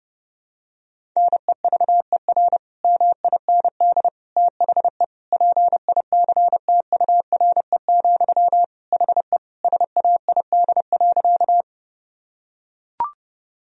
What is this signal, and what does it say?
Morse code, heard as a keyed tone, about 32 wpm, 700 Hz. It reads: NEVER MIND THE PICTURE, HE SAID.
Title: Sentences from Top 1000 Words - Review Part 1 - 30wpm